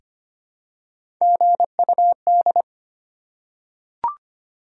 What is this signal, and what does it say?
Morse code, heard as a keyed tone, about 25 wpm, 700 Hz. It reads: GUD
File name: Top 20 QSO Elements - 4x Character Spacing - 25wpm